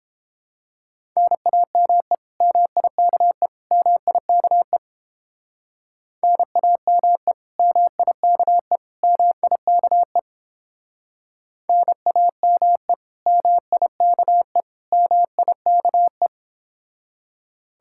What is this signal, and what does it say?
Morse code, heard as a keyed tone, about 29 wpm, 700 Hz. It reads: NAME MIKE MIKE NAME MIKE MIKE NAME MIKE MIKE